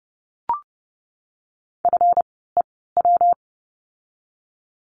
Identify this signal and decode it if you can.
Morse code, heard as a keyed tone, about 30 wpm, spaced as for 17 wpm, 700 Hz. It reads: FEW
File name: course / 30:10wpm / 237 Lesson 38f - Letter q - Words 30wpm